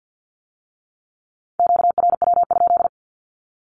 Morse code, heard as a keyed tone, about 50 wpm, 700 Hz. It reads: QRW?
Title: Q-Codes with Spoken Meaning - 50wpm